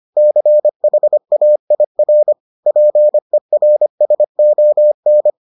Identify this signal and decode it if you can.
Morse code, heard as a keyed tone, about 25 wpm, 600 Hz. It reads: CHAIR PERSON